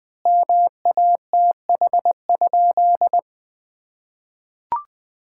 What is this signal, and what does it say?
Morse code, heard as a keyed tone, about 20 wpm, 700 Hz. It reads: MATH?
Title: Sentences from Top 1000 Words - Review Part 6 - 20wpm